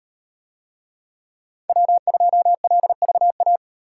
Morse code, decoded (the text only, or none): W2LVA